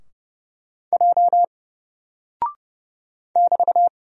Morse code, heard as a keyed tone, about 30 wpm, 700 Hz. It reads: J <BT>
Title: Single Character (Letter:Number:Punctuation:Prosign) Alt 30wpm